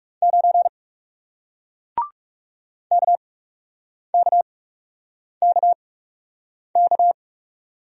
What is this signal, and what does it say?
Morse code, heard as a keyed tone, about 40 wpm, spaced as for 8 wpm, 700 Hz. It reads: 9 KKKK